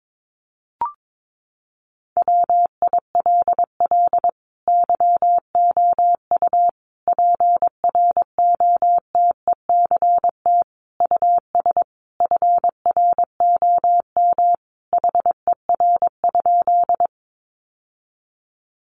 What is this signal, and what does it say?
Morse code, heard as a keyed tone, about 22 wpm, 700 Hz. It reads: WILL YOU PROTECT US FROM HER?
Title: Sentences from Top 900 Words 22wpm